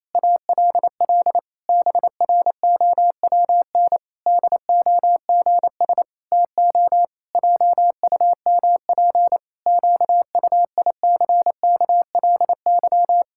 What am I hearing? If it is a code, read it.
Morse code, heard as a keyed tone, about 28 wpm, 700 Hz. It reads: ALL BROWN DOGS TO JUMP QUICKLY
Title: Pangram Challenge v2 - 28wpm